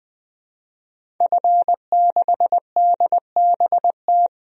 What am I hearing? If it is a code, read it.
Morse code, heard as a keyed tone, about 20 wpm, 700 Hz. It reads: F6DBT
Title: DXCC-100 Call Signs - 5 Characters - 20wpm